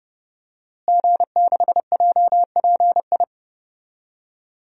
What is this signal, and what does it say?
Morse code, heard as a keyed tone, about 30 wpm, 700 Hz. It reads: G6JPI
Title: DXCC-100 Call Signs - 5 Characters - Repeated 3x - 30wpm